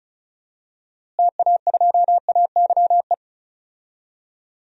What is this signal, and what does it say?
Morse code, heard as a keyed tone, about 36 wpm, 700 Hz. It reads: TA2AYE
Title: DXCC-100 Call Signs - 6 Characters - Repeated 2x - 35wpm